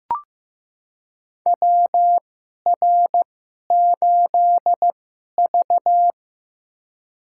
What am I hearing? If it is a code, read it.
Morse code, heard as a keyed tone, about 15 wpm, spaced as for 11 wpm, 700 Hz. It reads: WR8V
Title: USA Call Signs - 4 Characters - 2x Character Spacing - 15wpm